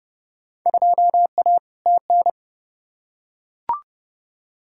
Morse code, heard as a keyed tone, about 32 wpm, 700 Hz. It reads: 2A TN
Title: Field Day Exchanges - 30wpm